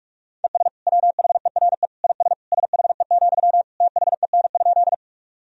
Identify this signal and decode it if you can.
Morse code, heard as a keyed tone, about 41 wpm, 700 Hz. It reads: ES WHERE IS SHE, THEN?
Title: Sentences from Top 200 Words and Abbreviations 45wpm